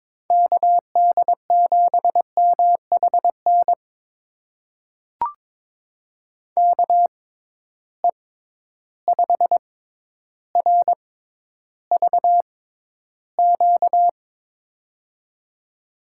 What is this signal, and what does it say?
Morse code, heard as a keyed tone, about 22 wpm, spaced as for 11 wpm, 700 Hz. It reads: KD7MHN KE5RVQ
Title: USA Call Signs - 6 Characters - 6x Character Spacing - 22wpm